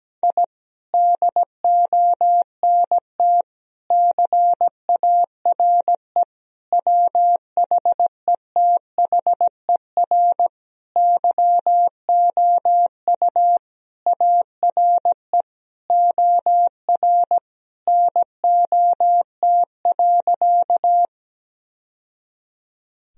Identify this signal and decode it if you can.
Morse code, heard as a keyed tone, about 17 wpm, 700 Hz. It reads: I DONT CARE WHETHER YOU ARE OR NOT.
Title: Sentences from Top 1000 Words - Review Part 3 - 17wpm